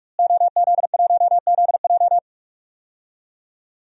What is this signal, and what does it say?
Morse code, heard as a keyed tone, about 46 wpm, 700 Hz. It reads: OZ1ZJ
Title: DXCC-100 Call Signs - 5 Characters - Repeated 3x - 45wpm